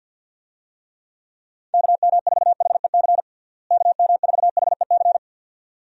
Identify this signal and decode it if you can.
Morse code, heard as a keyed tone, about 48 wpm, 700 Hz. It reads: KM4HEC KM4HEC
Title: USA Call Signs - 6 Characters - Repeated 2x - 50wpm